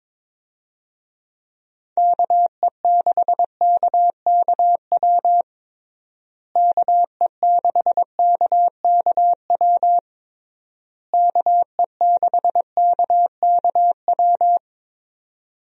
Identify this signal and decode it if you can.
Morse code, heard as a keyed tone, about 22 wpm, 700 Hz. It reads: KE6KKW KE6KKW KE6KKW
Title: USA Call Signs - 6 Characters - Repeated 3x - 22wpm